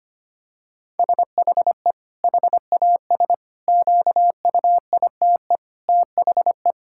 Morse code, heard as a keyed tone, about 25 wpm, 700 Hz. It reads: SHE HAS QUITE THE
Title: Sentences from Top 1000 Words - Review Part 5 - 25wpm